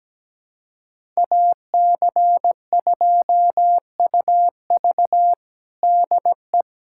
Morse code, heard as a keyed tone, about 17 wpm, 700 Hz. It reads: AC2UV DE
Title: USA Callsign DE Callsign - 5 Characters - 17wpm